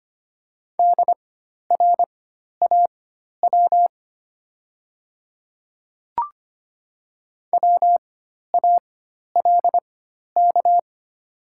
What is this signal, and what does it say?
Morse code, heard as a keyed tone, about 25 wpm, spaced as for 12 wpm, 700 Hz. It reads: DRAW WALK
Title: 4x Character Spacing - Top 300 Words - 25wpm